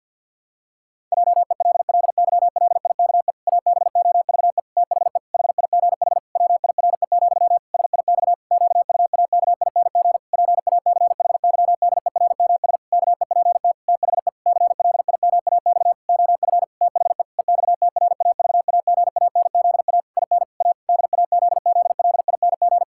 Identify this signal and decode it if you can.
Morse code, heard as a keyed tone, about 51 wpm, 700 Hz. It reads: JELLYLIKE ABOVE THE HIGH WIRE, SIX QUAKING PACHYDERMS KEPT THE CLIMAX OF THE EXTRAVAGANZA IN A DAZZLING
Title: Pangram Challenge v2 - 50wpm